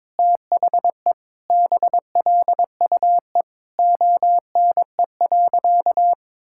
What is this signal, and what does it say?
Morse code, heard as a keyed tone, about 22 wpm, 700 Hz. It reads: THE BLUE ONE.
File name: practice / 22wpm / Sentences with 2-Word Phrases - 22wpm